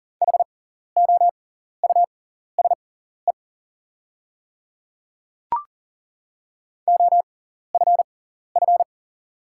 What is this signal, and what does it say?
Morse code, heard as a keyed tone, about 40 wpm, spaced as for 14 wpm, 690 Hz. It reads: HOUSE OFF
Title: 6x Character Spacing - Top 200 Words - 40wpm